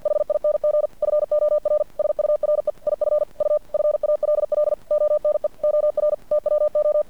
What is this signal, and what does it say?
Morse code, heard as a keyed tone, about 50 wpm, 600 Hz. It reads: HING FOR SURE IF U FALL ONE OR TWO
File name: archived / CRSnet